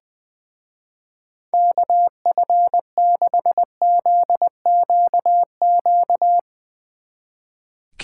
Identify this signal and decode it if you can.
Morse code, heard as a keyed tone, about 21 wpm, 700 Hz. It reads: KF6ZQQ
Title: Speed Racing - USA 2x3 Call Signs 20wpm